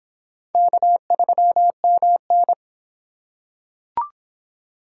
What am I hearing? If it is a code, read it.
Morse code, heard as a keyed tone, about 23 wpm, 700 Hz. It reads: K3MN